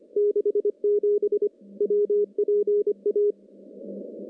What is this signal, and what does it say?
Morse code, heard as a keyed tone, about 25 wpm, 410 Hz. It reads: 67 WPA
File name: contests / CW-SS-k5tr- -0700